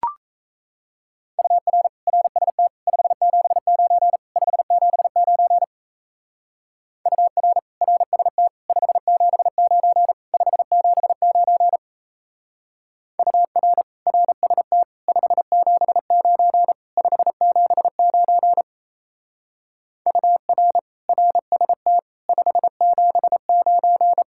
Morse code, heard as a keyed tone, about 33 wpm, 700 Hz. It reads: UR RST 579 579 UR RST 579 579 UR RST 579 579 UR RST 579
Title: Speed-Racing Common QSO Phrases - 28wpm